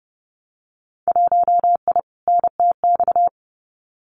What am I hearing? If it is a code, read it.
Morse code, heard as a keyed tone, about 30 wpm, 700 Hz. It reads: 1I NTX
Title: Winter Field Day Exchanges 30wpm